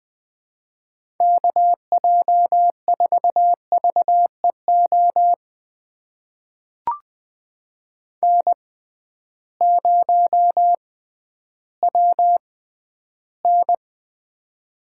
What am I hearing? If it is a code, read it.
Morse code, heard as a keyed tone, about 20 wpm, spaced as for 12 wpm, 700 Hz. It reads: KJ4VEO N0WN
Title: USA Call Signs - 6x Character Spacing - 20wpm